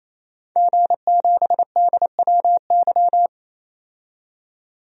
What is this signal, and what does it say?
Morse code, heard as a keyed tone, about 28 wpm, 700 Hz. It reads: G7DWY
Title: DXCC-100 Call Signs - 5 Characters - 28wpm